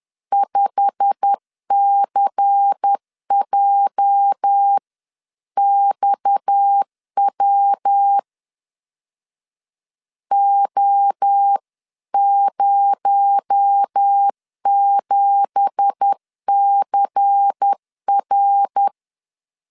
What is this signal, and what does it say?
Morse code, heard as a keyed tone, about 10 wpm, 800 Hz. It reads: 5CJXW O07CR